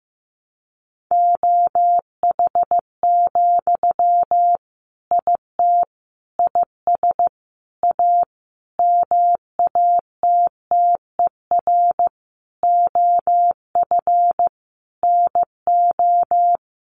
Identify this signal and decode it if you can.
Morse code, heard as a keyed tone, about 14 wpm, 700 Hz. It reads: OH, IT IS A MATTER OF NO
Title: Sentences from Top 1200 Words 15wpm